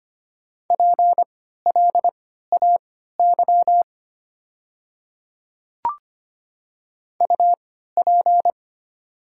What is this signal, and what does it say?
Morse code, heard as a keyed tone, about 25 wpm, spaced as for 14 wpm, 700 Hz. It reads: PLAY UP
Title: 3x Character Spacing - Top 200 Words - 25wpm